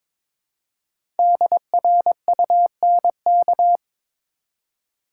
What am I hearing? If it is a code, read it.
Morse code, heard as a keyed tone, about 22 wpm, 700 Hz. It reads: DRUNK
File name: 5 Letter Words - First 3 Letters Given - 22wpm